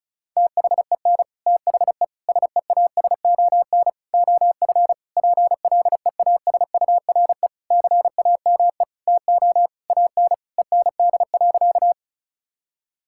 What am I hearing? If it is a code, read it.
Morse code, heard as a keyed tone, about 36 wpm, 700 Hz. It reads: THEN THE SEASON OF PLEASURE CAME TO AN END.
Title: Sentences from Top 1600 Words 35wpm